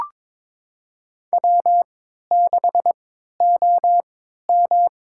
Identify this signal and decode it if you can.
Morse code, heard as a keyed tone, about 23 wpm, spaced as for 13 wpm, 690 Hz. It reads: W6OM
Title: USA Call Signs - 4 Characters - 3x Character Spacing - 22wpm